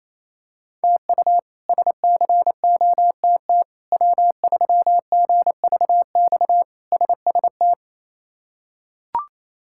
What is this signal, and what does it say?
Morse code, heard as a keyed tone, about 27 wpm, 700 Hz. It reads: TU SCOTT W3GVX SST
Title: K1USN SST QSO Practice - 28wpm